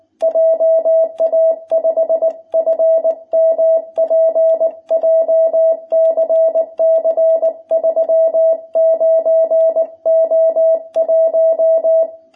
Morse code, heard as a keyed tone, about 20 wpm, 660 Hz. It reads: JA5FMP JCC 39O1
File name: jh8cbh / 2021cw